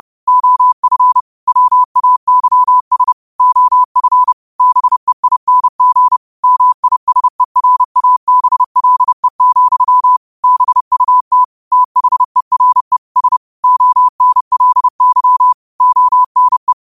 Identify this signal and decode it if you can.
Morse code, heard as a keyed tone, about 30 wpm, 1000 Hz. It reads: OF WAYS OF BEING MISERABLE, BUT THERE S ONLY ONE
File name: qotdfiles / 30wpm